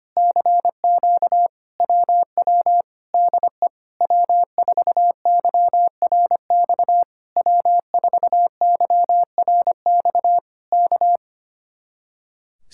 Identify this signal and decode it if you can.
Morse code, heard as a keyed tone, about 24 wpm, 700 Hz. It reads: CQ WW DE W4YRX W4YRX K
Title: Speed-Racing Common QSO Phrases Encore - 25wpm